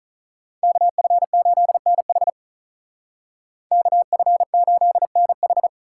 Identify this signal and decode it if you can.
Morse code, heard as a keyed tone, about 47 wpm, 700 Hz. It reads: KF8NH KF8NH